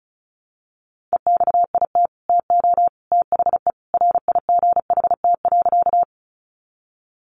Morse code, heard as a keyed tone, about 34 wpm, 700 Hz. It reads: EXIT TO THE RIGHT.
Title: Sentences from Top 1100 Words 35wpm